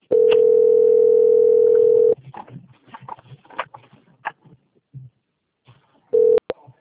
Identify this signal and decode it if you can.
Ringback tone (North America), 440 + 480 Hz.